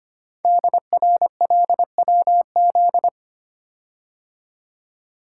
Morse code, heard as a keyed tone, about 25 wpm, 700 Hz. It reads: DRLWZ
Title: Code Groups - 5 Letters - 25wpm